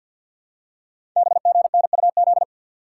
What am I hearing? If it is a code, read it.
Morse code, heard as a keyed tone, about 48 wpm, 700 Hz. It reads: BGNUZ
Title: Code Groups - 5 Letters - 50wpm